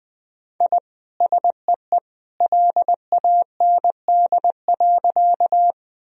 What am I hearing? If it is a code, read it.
Morse code, heard as a keyed tone, about 21 wpm, 700 Hz. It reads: I SEE LAND.
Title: Sentences from Top 1000 Words - Review Part 1 - 20wpm